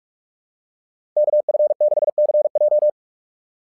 Morse code, heard as a keyed tone, about 44 wpm, 660 Hz. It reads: KF6CJ
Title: SOTA QSO Practice 45wpm